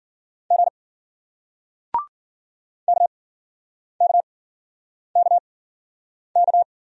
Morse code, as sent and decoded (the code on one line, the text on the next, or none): --. / -.- -.- -.- -.-
G KKKK